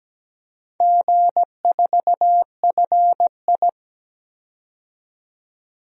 Morse code, heard as a keyed tone, about 17 wpm, 700 Hz. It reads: G4FI
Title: DXCC-100 Call Signs - 4 Characters - 17wpm